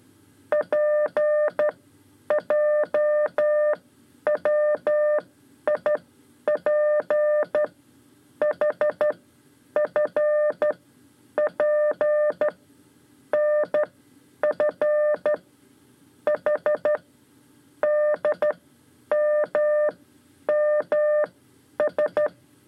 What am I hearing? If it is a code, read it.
Morse code, heard as a keyed tone, about 13 wpm, spaced as for 9 wpm, 600 Hz. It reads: PJWIPHFP NF H DMMS
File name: 5 Lezione CW 2